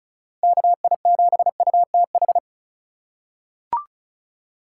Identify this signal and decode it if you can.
Morse code, heard as a keyed tone, about 34 wpm, 700 Hz. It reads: KI7UTH
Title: USA Call Signs - 2x Character Spacing - 35wpm